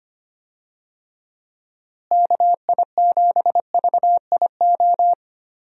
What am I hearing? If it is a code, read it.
Morse code, heard as a keyed tone, about 25 wpm, 700 Hz. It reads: KI7VIO
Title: USA Call Signs - 6 Characters - Repeated 3x - 25wpm